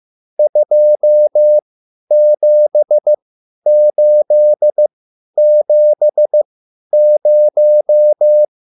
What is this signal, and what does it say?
Morse code, heard as a keyed tone, about 15 wpm, spaced as for 10 wpm, 600 Hz. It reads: 27870